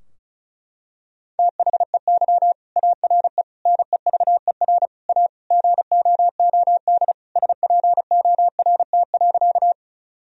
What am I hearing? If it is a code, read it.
Morse code, heard as a keyed tone, about 36 wpm, 700 Hz. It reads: THEY ARE NEVER A GOOD SPORT.
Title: Sentences from Top 1000 Words - Review Part 4 - 35wpm